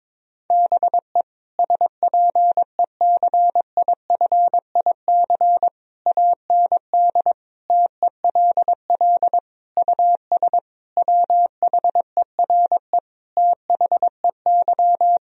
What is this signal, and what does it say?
Morse code, heard as a keyed tone, about 22 wpm, 700 Hz. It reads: BE SPECIFIC AND TELL US WHERE THEY
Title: Sentences from Top 1000 Words - Review Part 3 - 22wpm